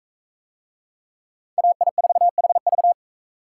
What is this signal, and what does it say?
Morse code, heard as a keyed tone, about 39 wpm, 700 Hz. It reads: AI4HV